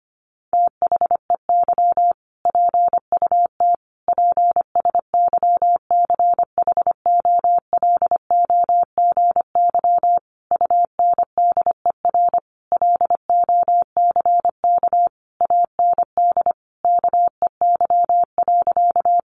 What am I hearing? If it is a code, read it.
Morse code, heard as a keyed tone, about 25 wpm, 700 Hz. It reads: THEY PUT PSYCHOLOGY UNDER LOCK AND KEY.
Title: Sentences from Top 1400 Words 25wpm